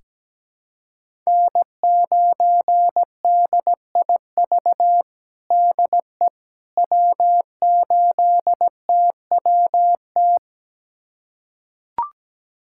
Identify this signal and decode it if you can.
Morse code, heard as a keyed tone, about 17 wpm, 700 Hz. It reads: N9DIV DE W8TWT
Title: USA Callsign DE Callsign - 5 Characters - 17wpm